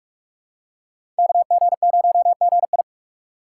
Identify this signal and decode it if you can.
Morse code, heard as a keyed tone, about 46 wpm, 700 Hz. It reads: KG0GI
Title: USA Call Signs - 5 Characters - 2x Character Spacing - 45wpm